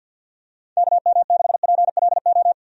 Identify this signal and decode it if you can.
Morse code, heard as a keyed tone, about 51 wpm, 700 Hz. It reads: KM6PLO
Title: USA Call Signs - 6 Characters - 50wpm